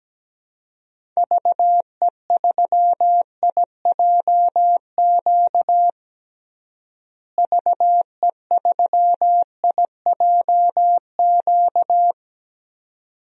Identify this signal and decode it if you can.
Morse code, heard as a keyed tone, about 17 wpm, 700 Hz. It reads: VE3IJQ VE3IJQ
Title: DXCC-100 Call Signs - 6 Characters - Repeated 2x - 17wpm